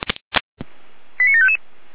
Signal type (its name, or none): five-tone selective call